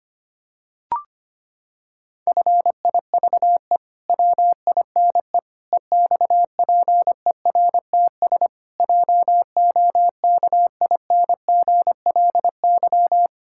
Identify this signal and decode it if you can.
Morse code, heard as a keyed tone, about 25 wpm, 700 Hz. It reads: FIVE WINE EXPERTS JOKINGLY
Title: Pangram Challenge v2 - 25wpm